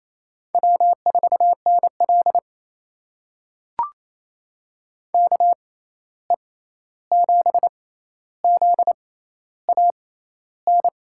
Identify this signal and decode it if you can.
Morse code, heard as a keyed tone, about 28 wpm, spaced as for 13 wpm, 700 Hz. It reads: W4NL KE7ZAN
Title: USA Call Signs - 6x Character Spacing - 28wpm